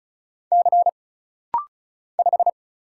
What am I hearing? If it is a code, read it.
Morse code, heard as a keyed tone, about 34 wpm, 700 Hz. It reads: C 5
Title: Single Letter-Number Alt - Rapid-Fire 35wpm